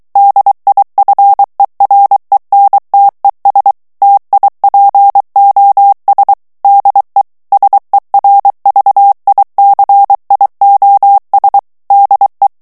Morse code, heard as a keyed tone, about 23 wpm, 800 Hz. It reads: DIFERENTES TIPOS DE SERVICIOS DE
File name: morse mp3 / ESPAGNOL TEXTE EN CLAIR 2 VITESSE 22 MOTS MINUTES